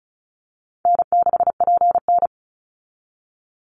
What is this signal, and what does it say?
Morse code, heard as a keyed tone, about 34 wpm, 700 Hz. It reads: N6PN
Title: Easy Call Signs Repeated 3x - 35wpm